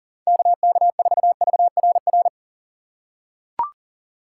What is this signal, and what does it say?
Morse code, heard as a keyed tone, about 40 wpm, 700 Hz. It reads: KK4VRR
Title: USA Call Signs - 6 Characters - 4x Character Spacing - 40wpm